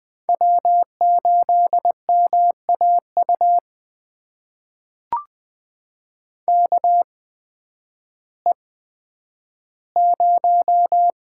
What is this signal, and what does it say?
Morse code, heard as a keyed tone, about 20 wpm, spaced as for 11 wpm, 700 Hz. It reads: W8MAU KE0
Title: USA Call Signs - 5 Characters - 8x Character Spacing - 20wpm